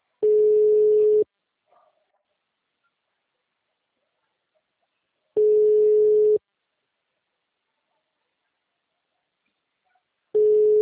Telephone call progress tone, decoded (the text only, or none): ringback tone